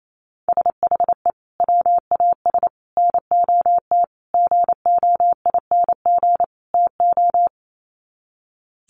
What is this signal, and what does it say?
Morse code, heard as a keyed tone, about 28 wpm, 700 Hz. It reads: SHE WAS NOT GOING TO
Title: Sets of 5 Words Encore2 28wpm